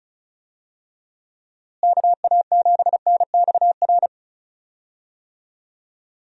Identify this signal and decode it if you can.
Morse code, heard as a keyed tone, about 36 wpm, 700 Hz. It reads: KA7NXR
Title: USA Call Signs - 6 Characters - Repeated 2x - 35wpm